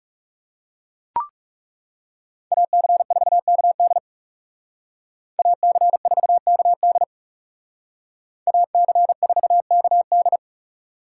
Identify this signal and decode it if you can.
Morse code, heard as a keyed tone, about 35 wpm, 700 Hz. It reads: AC4KD AC4KD AC4KD
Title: Speed Racing - USA 2x2 Call Signs 30wpm